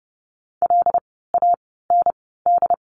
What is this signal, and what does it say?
Morse code, heard as a keyed tone, about 30 wpm, spaced as for 17 wpm, 700 Hz. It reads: LAND
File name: Lesson 17f - Letter u - Words 30wpm